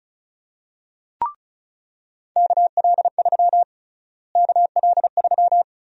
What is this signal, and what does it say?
Morse code, heard as a keyed tone, about 36 wpm, 700 Hz. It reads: KL3 KL3
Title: USA Call Sign Prefixes - Repeated 2x - 35wpm